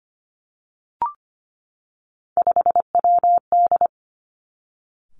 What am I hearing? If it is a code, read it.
Morse code, heard as a keyed tone, about 25 wpm, 700 Hz. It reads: 5WD
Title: Three Challenging Letter-Number Combinations 25wpm